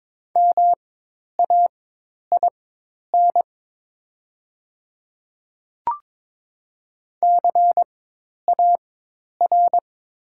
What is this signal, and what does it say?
Morse code, heard as a keyed tone, about 22 wpm, spaced as for 10 wpm, 700 Hz. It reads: MAIN CAR
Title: 4x Character Spacing - Top 300 Words - 22wpm